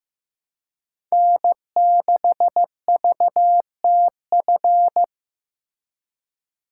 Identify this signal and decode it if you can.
Morse code, heard as a keyed tone, about 18 wpm, 700 Hz. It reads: N6VTF
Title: Speed Racing - USA 1x3 Call Signs 15wpm